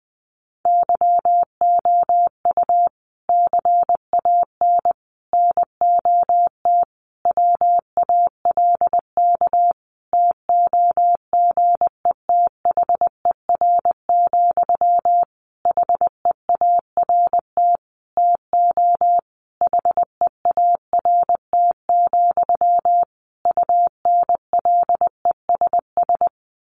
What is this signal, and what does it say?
Morse code, heard as a keyed tone, about 19 wpm, 700 Hz. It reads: YOU CAN NOT WALK TOGETHER, HEART TO HEART, UNLESS
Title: Sentences from Top 1500 Words 20wpm